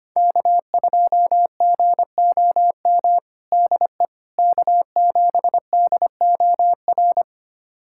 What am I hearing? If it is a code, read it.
Morse code, heard as a keyed tone, about 25 wpm, 700 Hz. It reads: K2GOM DE K7DOR
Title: USA Callsign DE Callsign - 5 Characters - 25wpm